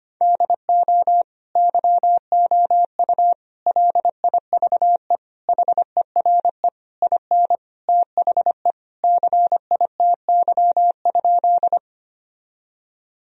Morse code, heard as a keyed tone, about 25 wpm, 700 Hz. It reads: DO YOU LIVE HERE IN THE CITY?
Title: Sentences from Top 1000 Words - Review Part 1 - 25wpm